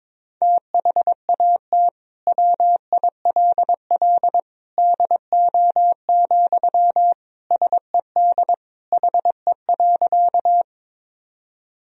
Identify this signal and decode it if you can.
Morse code, heard as a keyed tone, about 22 wpm, 700 Hz. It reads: THAT WILL DO, SED HE.
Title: Sentences from Top 100 Words and Abbreviations 22wpm